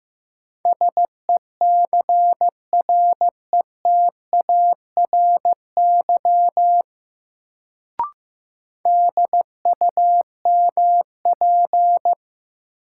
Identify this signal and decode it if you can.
Morse code, heard as a keyed tone, about 15 wpm, 700 Hz. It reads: SECRETARY DUMP
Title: Sentences from Top 1300 Words - New Word Drill - 15wpm